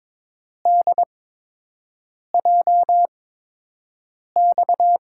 Morse code, heard as a keyed tone, about 22 wpm, spaced as for 6 wpm, 700 Hz. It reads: DJX
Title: USA Call Signs - 8x Character Spacing - 22wpm